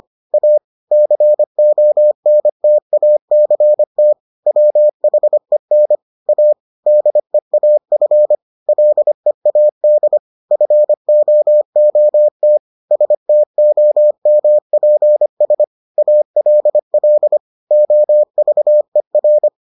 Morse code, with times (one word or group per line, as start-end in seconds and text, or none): 0.3-0.6 A
0.9-4.1 CONTACT
4.5-6.0 WHEN
6.3-6.5 A
6.9-8.4 DEAF
8.7-10.2 LEAD
10.5-12.6 FOOT
12.9-15.6 STOMPS
16.0-17.4 ALL
17.7-19.5 OVER